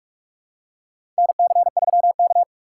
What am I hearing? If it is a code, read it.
Morse code, heard as a keyed tone, about 44 wpm, 700 Hz. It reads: NC3K
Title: USA Call Signs - 4 Characters - 4x Character Spacing - 45wpm